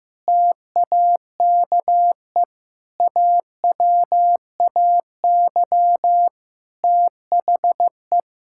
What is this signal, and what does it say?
Morse code, heard as a keyed tone, about 15 wpm, 700 Hz. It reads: TAKE AWAY THE
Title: Sentences from Top 1600 Words 15wpm